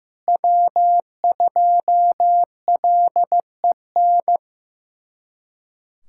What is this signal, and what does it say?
Morse code, heard as a keyed tone, about 15 wpm, 700 Hz. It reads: W2LEN
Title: USA Callsign DE Callsign 15wpm